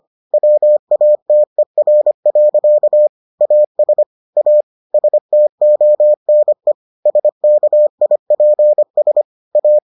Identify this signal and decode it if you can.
Morse code, heard as a keyed tone, about 25 wpm, 600 Hz. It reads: WATER. AS A STONE SKIPS A